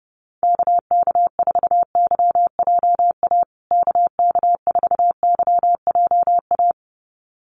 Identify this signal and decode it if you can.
Morse code, heard as a keyed tone, about 30 wpm, 700 Hz. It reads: KK4YJA KK4YJA
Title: USA Call Signs Repeated 2x 30wpm